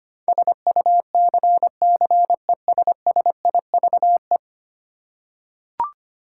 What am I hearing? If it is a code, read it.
Morse code, heard as a keyed tone, about 25 wpm, 700 Hz. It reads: SUCCESSIVE
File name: 10 Letter Words - First 3 Letters Given - 25wpm